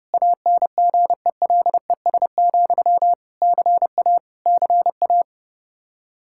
Morse code, heard as a keyed tone, about 31 wpm, 700 Hz. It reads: ANGELES, CA CA
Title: K1USN SST QSO Practice 30wpm